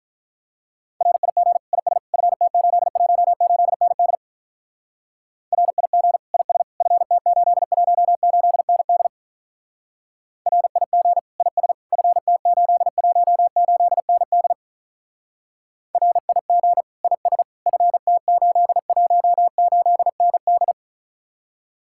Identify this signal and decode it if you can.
Morse code, heard as a keyed tone, about 45 wpm, 700 Hz. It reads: RIG IS FT818ND RIG IS FT818ND RIG IS FT818ND RIG IS FT818ND